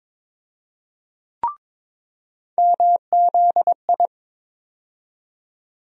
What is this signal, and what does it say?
Morse code, heard as a keyed tone, about 22 wpm, 700 Hz. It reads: MZI
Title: USA Call Sign Postfixes - 3 Letters 22wpm